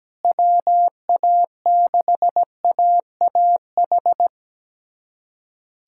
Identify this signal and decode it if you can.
Morse code, heard as a keyed tone, about 17 wpm, 700 Hz. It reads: WA6AAH
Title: USA Callsign DE Callsign 17wpm